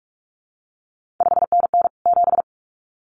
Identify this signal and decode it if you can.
Morse code, heard as a keyed tone, about 46 wpm, 700 Hz. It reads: 5NN 7
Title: CQ WW DX Contest Exchange Encore - 45wpm